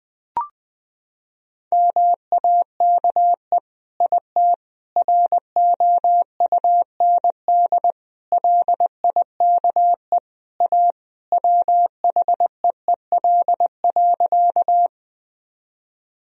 Morse code, heard as a keyed tone, about 20 wpm, 700 Hz. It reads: MAKE IT ROUND LIKE A WHEEL.
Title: Sentences from Top 1000 Words - Review Part 3 - 20wpm